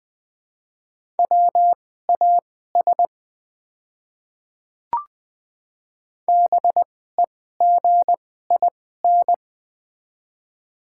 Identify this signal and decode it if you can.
Morse code, heard as a keyed tone, about 20 wpm, spaced as for 14 wpm, 700 Hz. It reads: WAS BEGIN
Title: 2x Character Spacing - Top 300 Words - 20wpm